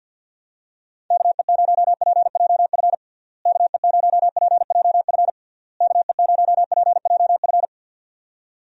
Morse code, heard as a keyed tone, about 50 wpm, 700 Hz. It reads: KE0PJF KE0PJF KE0PJF
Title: USA Call Signs - 6 Characters - Repeated 3x - 50wpm